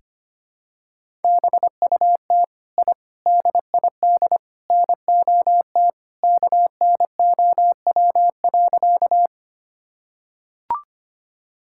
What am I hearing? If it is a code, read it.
Morse code, heard as a keyed tone, about 25 wpm, 700 Hz. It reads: BUT I DID NOT KNOW.
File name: practice / 25wpm / Sentences with 4-Word Phrases - 25wpm